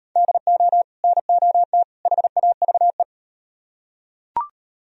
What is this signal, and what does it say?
Morse code, heard as a keyed tone, about 38 wpm, 700 Hz. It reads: DO NOT HAVE